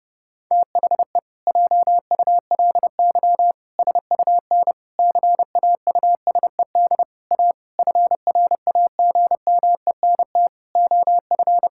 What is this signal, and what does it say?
Morse code, heard as a keyed tone, about 32 wpm, 700 Hz. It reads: THE JULY SUN CAUSED A FRAGMENT OF
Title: Pangram Challenge v2 - 30wpm